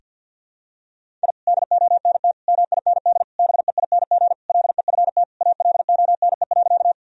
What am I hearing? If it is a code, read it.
Morse code, heard as a keyed tone, about 48 wpm, 700 Hz. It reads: I DONT MIND BEING LEFT ALONE.
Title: Sentences from Top 1000 Words - Review Part 3 - 50wpm